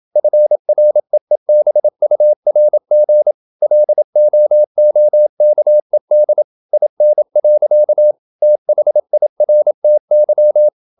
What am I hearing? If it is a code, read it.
Morse code, heard as a keyed tone, about 27 wpm, 600 Hz. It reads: FREEBURG LOOKED IN. THIRTY
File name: Never 0027wpm